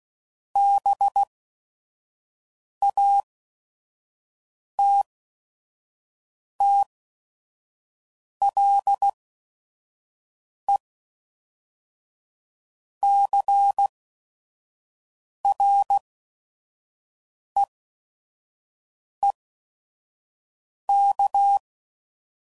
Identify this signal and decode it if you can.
Morse code, heard as a keyed tone, about 16 wpm, spaced as for 5 wpm, 800 Hz. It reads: BATTLECREEK
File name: CW Code Groups @5WPM / fromSM's / Cities